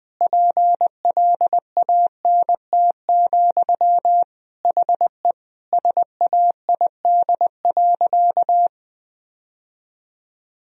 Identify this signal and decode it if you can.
Morse code, heard as a keyed tone, about 21 wpm, 700 Hz. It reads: PLANT, HE SAID.
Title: Sentences from Top 1000 Words - Review Part 1 - 20wpm